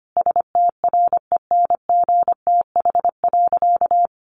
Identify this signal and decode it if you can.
Morse code, heard as a keyed tone, about 25 wpm, 700 Hz. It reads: STRENGTH.
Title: Sentences from Top 1300 Words 25wpm